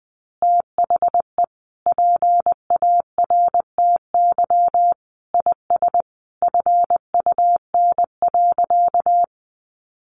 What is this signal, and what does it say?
Morse code, heard as a keyed tone, about 20 wpm, 700 Hz. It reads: THE PARTY IS FUN.
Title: Easy Phrases 20wpm